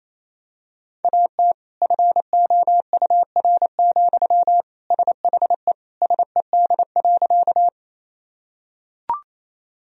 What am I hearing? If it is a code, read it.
Morse code, heard as a keyed tone, about 28 wpm, 700 Hz. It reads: AT FOUR, SHE SED.
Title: Sentences from Top 300 Words and Abbreviations 28wpm